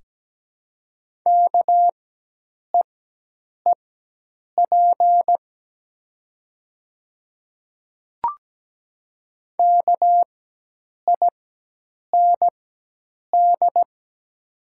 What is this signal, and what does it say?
Morse code, heard as a keyed tone, about 17 wpm, spaced as for 8 wpm, 700 Hz. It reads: KEEP KIND
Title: 4x Character Spacing - Top 500 Words - 17wpm